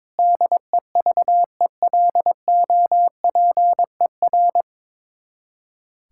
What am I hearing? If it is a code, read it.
Morse code, heard as a keyed tone, about 22 wpm, 700 Hz. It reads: DEVELOPER
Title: Practice QSO - Exchange 3 - More Info - 22wpm